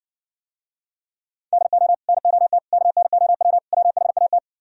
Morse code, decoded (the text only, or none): DO NOT KNOW WHAT